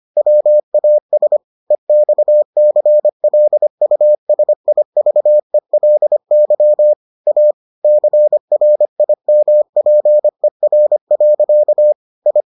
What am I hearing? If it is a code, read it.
Morse code, heard as a keyed tone, about 25 wpm, 600 Hz. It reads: WAS EXCLUSIVELY A CRIMPER. I